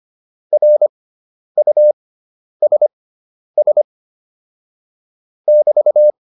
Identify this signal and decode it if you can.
Morse code, heard as a keyed tone, about 25 wpm, spaced as for 10 wpm, 600 Hz. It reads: RUSS <BT>
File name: QSO207 10